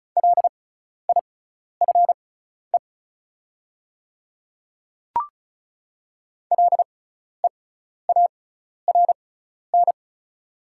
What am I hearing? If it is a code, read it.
Morse code, heard as a keyed tone, about 36 wpm, spaced as for 12 wpm, 700 Hz. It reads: LIFE LEARN
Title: 6x Character Spacing - Top 300 Words - 35wpm